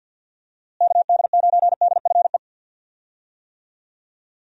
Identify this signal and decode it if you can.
Morse code, heard as a keyed tone, about 52 wpm, 700 Hz. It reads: KD9DFE
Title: USA Call Signs - 6 Characters - Repeated 2x - 50wpm